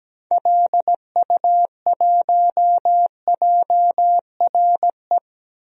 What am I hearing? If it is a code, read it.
Morse code, heard as a keyed tone, about 17 wpm, 700 Hz. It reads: LU1JRE
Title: DXCC-100 Call Signs - 6 Characters - Repeated 3x - 17wpm